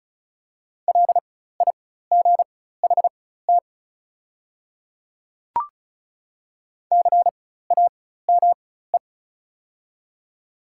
Morse code, heard as a keyed tone, about 36 wpm, spaced as for 16 wpm, 700 Hz. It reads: LIGHT CAME
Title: 4x Character Spacing - Top 200 Words - 35wpm